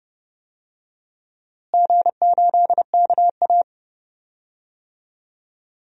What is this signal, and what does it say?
Morse code, heard as a keyed tone, about 30 wpm, 700 Hz. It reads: G8KA
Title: DXCC-100 Call Signs - 4 Characters - Repeated 2x - 30wpm